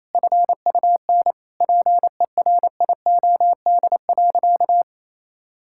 Morse code, heard as a keyed tone, about 28 wpm, 700 Hz. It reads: FUN PERIOD.
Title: Sentences from Top 1000 Words - Review Part 5 - 28wpm